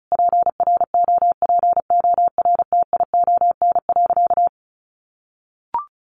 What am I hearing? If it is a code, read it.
Morse code, heard as a keyed tone, about 34 wpm, 700 Hz. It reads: PROPORTION.
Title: A Sound Of Thunder 35wpm